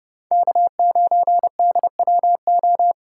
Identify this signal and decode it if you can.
Morse code, heard as a keyed tone, about 40 wpm, 700 Hz. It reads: K9DWO